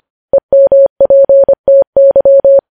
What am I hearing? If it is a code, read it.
Morse code, heard as a keyed tone, about 25 wpm, 550 Hz. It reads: EMPTY
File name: podcast / News headlines-1x-25x25